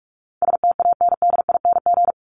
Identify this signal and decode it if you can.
Morse code, heard as a keyed tone, about 44 wpm, 700 Hz. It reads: STANDING
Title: ING Suffix - Top 100 Words with Base Words 45wpm